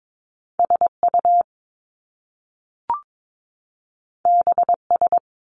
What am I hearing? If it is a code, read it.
Morse code, heard as a keyed tone, about 22 wpm, 700 Hz. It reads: SU BS
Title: Top Two Letter Combinations in Words 22wpm